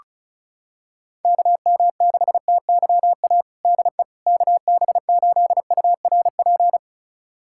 Morse code, heard as a keyed tone, about 34 wpm, 700 Hz. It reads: KM6TYA DE KB8URP
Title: USA Callsign DE Callsign 35wpm